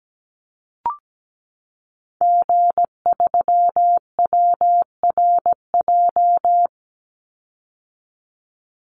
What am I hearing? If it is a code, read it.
Morse code, heard as a keyed tone, about 17 wpm, 700 Hz. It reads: G3WRJ
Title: Call Signs - 17wpm